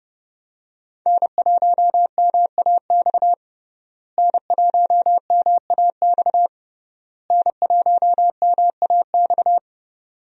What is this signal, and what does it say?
Morse code, heard as a keyed tone, about 30 wpm, 700 Hz. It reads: N1MAX N1MAX N1MAX
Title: USA Call Signs - 5 Characters - Repeated 3x - 30wpm